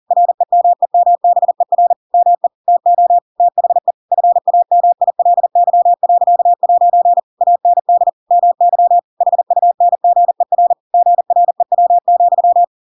Morse code, heard as a keyed tone, about 40 wpm, 700 Hz. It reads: REMEMBER ME TO THE FAMILY.' AND MY HUNGER GREW,
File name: Pinocchio / Pinocchio ch7 40 40 ik1hge